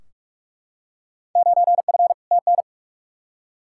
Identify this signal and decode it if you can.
Morse code, heard as a keyed tone, about 45 wpm, 700 Hz. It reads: 9F TN
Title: Field Day Exchanges Encore - 45wpm